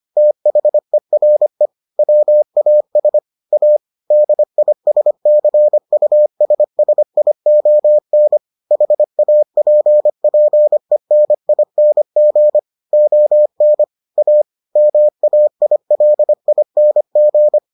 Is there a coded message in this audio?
Yes: Morse code from 0.2 to 17.6 s, heard as a keyed tone, about 25 wpm, 600 Hz. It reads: THERE WAS A DISCUSSION HAPPENING ON A MAILING